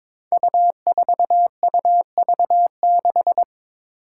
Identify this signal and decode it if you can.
Morse code, heard as a keyed tone, about 22 wpm, 700 Hz. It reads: U4UV6
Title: Five Challenging Letter-Number Combinations 22wpm